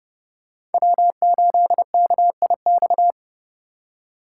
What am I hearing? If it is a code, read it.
Morse code, heard as a keyed tone, about 30 wpm, 700 Hz. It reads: W8KIX